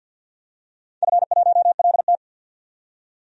Morse code, heard as a keyed tone, about 52 wpm, 700 Hz. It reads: F1LT
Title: DXCC-100 Call Signs - 4 Characters - 50wpm